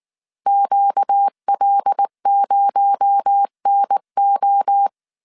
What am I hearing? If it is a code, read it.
Morse code, heard as a keyed tone, about 19 wpm, 800 Hz. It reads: QL0NO
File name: CW / DET / 1 9 1500C S A 18wpm 2459